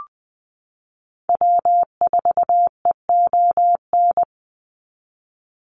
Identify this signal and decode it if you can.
Morse code, heard as a keyed tone, about 20 wpm, 700 Hz. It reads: W4EON
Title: NA SOTA Activator Callsigns 20wpm